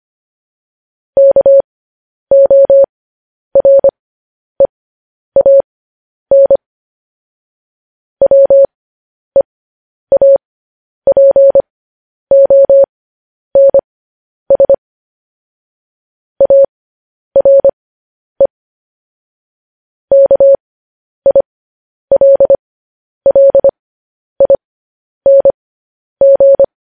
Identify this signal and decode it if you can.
Morse code, heard as a keyed tone, about 25 wpm, spaced as for 10 wpm, 550 Hz. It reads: KOREAN WEAPONS ARE KILLING